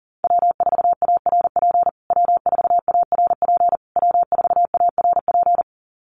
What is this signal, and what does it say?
Morse code, heard as a keyed tone, about 37 wpm, 700 Hz. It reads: W4ARP W4ARP W4ARP
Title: USA Call Signs Repeated 3x 40wpm